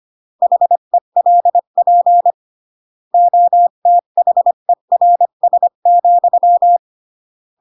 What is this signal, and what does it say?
Morse code, heard as a keyed tone, about 25 wpm, 700 Hz. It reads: HELP OTHERS,